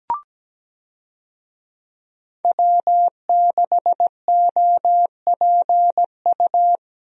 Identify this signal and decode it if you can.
Morse code, heard as a keyed tone, about 17 wpm, 700 Hz. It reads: W6OPU
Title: USA Call Signs - 5 Characters - Repeated 2x - 17wpm